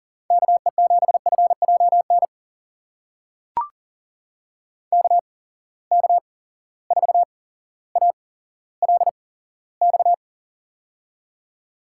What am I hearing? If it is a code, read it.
Morse code, heard as a keyed tone, about 42 wpm, spaced as for 17 wpm, 700 Hz. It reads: KE7FJN KK4ALX
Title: USA Call Signs - 6 Characters - 8x Character Spacing - 40wpm